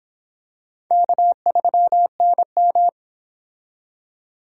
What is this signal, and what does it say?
Morse code, heard as a keyed tone, about 25 wpm, 700 Hz. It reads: K3NM